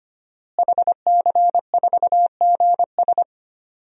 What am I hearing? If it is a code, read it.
Morse code, heard as a keyed tone, about 25 wpm, 700 Hz. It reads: HC4GS
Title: DXCC-100 Call Signs - 5 Characters - Repeated 3x - 25wpm